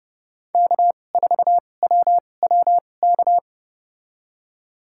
Morse code, heard as a keyed tone, about 29 wpm, spaced as for 22 wpm, 700 Hz. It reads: K4WWK
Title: USA Call Signs - 5 Characters - 2x Character Spacing - 30wpm